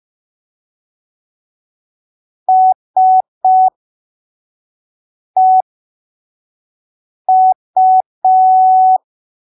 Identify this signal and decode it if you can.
Morse code, heard as a keyed tone, about 5 wpm, spaced as for 3 wpm, 750 Hz. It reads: SEU